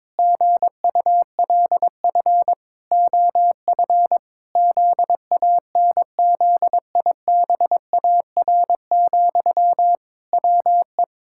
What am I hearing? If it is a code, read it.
Morse code, heard as a keyed tone, about 22 wpm, 700 Hz. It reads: GULF OF ZANZIBAR, WE
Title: Pangram Challenge v2 - 22wpm